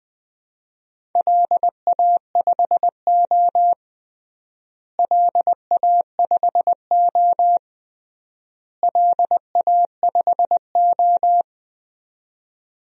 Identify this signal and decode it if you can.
Morse code, heard as a keyed tone, about 20 wpm, 700 Hz. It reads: LA5O LA5O LA5O
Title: DXCC-100 Call Signs - 4 Characters - Repeated 3x - 20wpm